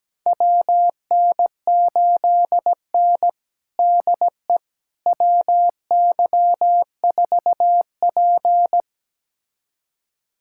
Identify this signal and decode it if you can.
Morse code, heard as a keyed tone, about 17 wpm, 700 Hz. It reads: WN8N DE WY4P
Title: USA Callsign DE Callsign - 4 Characters - 17wpm